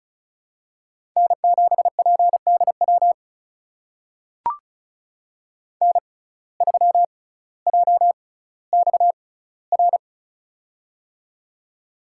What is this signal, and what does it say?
Morse code, heard as a keyed tone, about 36 wpm, spaced as for 18 wpm, 700 Hz. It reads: N7PDW N3JXR
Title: USA Call Signs - 5 Characters - 6x Character Spacing - 35wpm